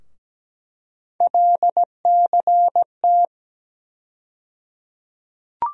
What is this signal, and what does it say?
Morse code, heard as a keyed tone, about 17 wpm, 700 Hz. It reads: LCT
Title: USA Call Sign Postfixes - 3 Letters 17wpm